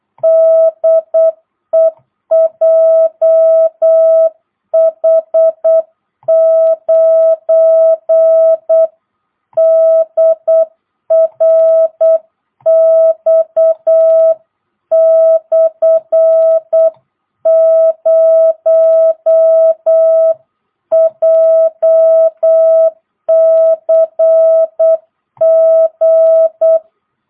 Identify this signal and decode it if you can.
Morse code, heard as a keyed tone, about 8 wpm, 660 Hz. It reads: DE JH9DRX/0 JCG
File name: jh8cbh / 2022cw